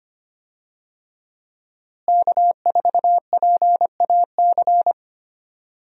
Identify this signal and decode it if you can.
Morse code, heard as a keyed tone, about 25 wpm, 700 Hz. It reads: K4PAC
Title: USA Call Signs - 5 Characters - Repeated 3x - 25wpm